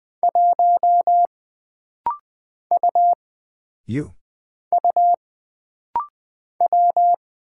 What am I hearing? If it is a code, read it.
Morse code, heard as a keyed tone, about 20 wpm, 700 Hz. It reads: 1 U U W
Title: Single Character (Letter:Number:Punctuation:Prosign) - Rapid-Fire - 20wpm